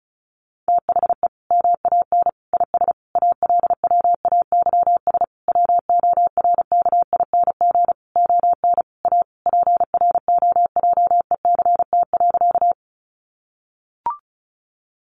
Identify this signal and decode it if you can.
Morse code, heard as a keyed tone, about 34 wpm, 700 Hz. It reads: THE MAN IS ALWAYS WORKING ON A PROJECT.
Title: Sentences from Top 1400 Words 35wpm